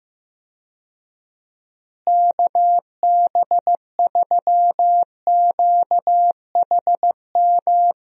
Morse code, heard as a keyed tone, about 15 wpm, 700 Hz. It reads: KB3QHM
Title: USA Call Signs - 6 Characters - Repeated 3x - 15wpm